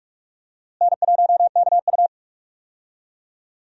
Morse code, heard as a keyed tone, about 46 wpm, 700 Hz. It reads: N1KU
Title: USA Call Signs - 4 Characters - Repeated 2x - 45wpm